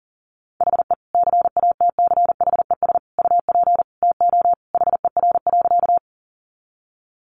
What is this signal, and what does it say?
Morse code, heard as a keyed tone, about 40 wpm, 700 Hz. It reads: HE CATCHES UP TO HER.
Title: Additional Easy Phrases 40wpm